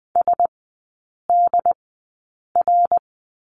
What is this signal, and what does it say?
Morse code, heard as a keyed tone, about 20 wpm, spaced as for 8 wpm, 700 Hz. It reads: SDR
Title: 172 Lesson 28e - Number 9 - Three Characters 20wpm